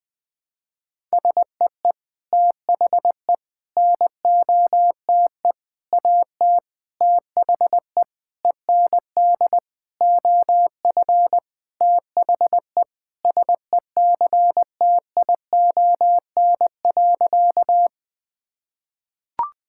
Morse code, heard as a keyed tone, about 21 wpm, 700 Hz. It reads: SEE THE NOTE AT THE END OF THE SECTION.
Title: Sentences from Top 1000 Words - Review Part 5 - 20wpm